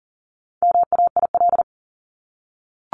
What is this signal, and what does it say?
Morse code, heard as a keyed tone, about 40 wpm, 700 Hz. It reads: MAIL
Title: Top 1751-2000 Words 40wpm